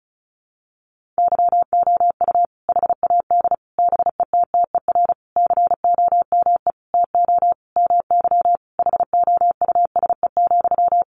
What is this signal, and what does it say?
Morse code, heard as a keyed tone, about 34 wpm, 700 Hz. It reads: YOU HAD BETTER COME TO MY HOUSE,
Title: Sentences from Top 400 Words 35wpm